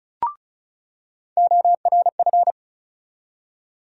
Morse code, heard as a keyed tone, about 34 wpm, 700 Hz. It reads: ORF
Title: USA Call Sign Postfixes - 3 Letters 35wpm